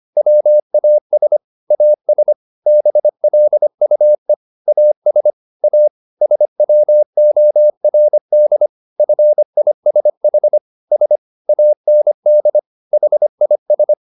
Morse code, heard as a keyed tone, about 25 wpm, 600 Hz. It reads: WAS AS BLUE AS A SWORD FISH S AND HIS